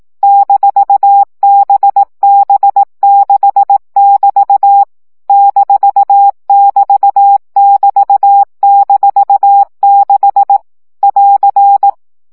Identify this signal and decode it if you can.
Morse code, heard as a keyed tone, about 18 wpm, 800 Hz. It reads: -BB6<BT> -<BT><BT>-6 <AR>
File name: CW gyakorlas b 6 egyenlo minusz 15 baud